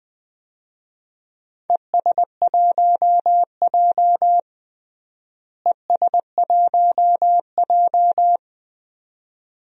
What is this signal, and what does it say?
Morse code, heard as a keyed tone, about 20 wpm, 700 Hz. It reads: ES1J ES1J
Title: DXCC-100 Call Signs - 4 Characters - Repeated 2x - 20wpm